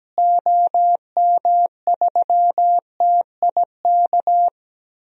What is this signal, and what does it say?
Morse code, heard as a keyed tone, about 17 wpm, 700 Hz. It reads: OM3TIK
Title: DXCC-100 Call Signs - 6 Characters - Repeated 3x - 17wpm